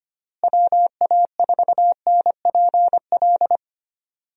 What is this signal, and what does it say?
Morse code, heard as a keyed tone, about 25 wpm, 700 Hz. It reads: WA4NPL
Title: USA Call Signs - 6 Characters - Repeated 3x - 25wpm